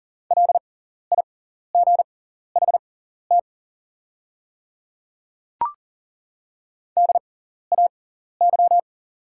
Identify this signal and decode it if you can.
Morse code, heard as a keyed tone, about 42 wpm, spaced as for 14 wpm, 690 Hz. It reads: LIGHT DAY
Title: 6x Character Spacing - Top 200 Words - 40wpm